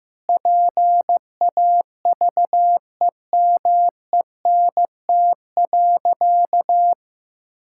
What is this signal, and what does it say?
Morse code, heard as a keyed tone, about 15 wpm, 700 Hz. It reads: PAVEMENT.
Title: Pangram Challenge v2 - 15wpm